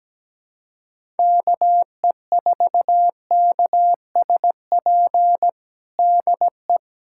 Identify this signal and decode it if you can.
Morse code, heard as a keyed tone, about 17 wpm, 700 Hz. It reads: KE4KSP DE
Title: USA Callsign DE Callsign 17wpm